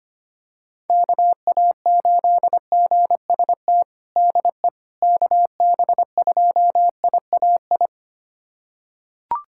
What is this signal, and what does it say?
Morse code, heard as a keyed tone, about 25 wpm, 700 Hz. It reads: KA8GST DE KB2IAI
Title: USA Callsign DE Callsign - 6 Characters - 25wpm